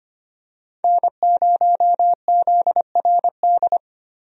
Morse code, heard as a keyed tone, about 25 wpm, 700 Hz. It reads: N0ZRD
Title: USA Call Signs - 5 Characters - Repeated 2x - 25wpm